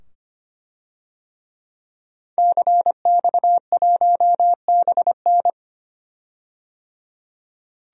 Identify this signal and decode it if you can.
Morse code, heard as a keyed tone, about 25 wpm, 700 Hz. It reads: CX1BN
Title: DXCC-100 Call Signs - 5 Characters - 25wpm